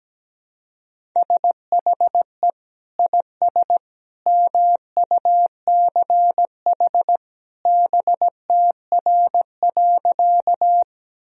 Morse code, heard as a keyed tone, about 17 wpm, 700 Hz. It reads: SHE IS MUCH BTR.
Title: Sentences from Top 400 Words and Abbreviations 17wpm